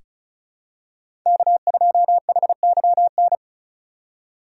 Morse code, heard as a keyed tone, about 34 wpm, 700 Hz. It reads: K2HYN
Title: USA Call Signs - 5 Characters - 4x Character Spacing - 35wpm